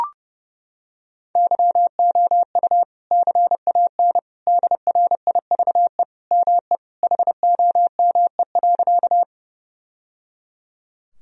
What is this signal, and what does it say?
Morse code, heard as a keyed tone, about 32 wpm, 700 Hz. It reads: YOU CAN DRIVE ME HOME.
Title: Sentences from Top 1000 Words - Review Part 2 - 30wpm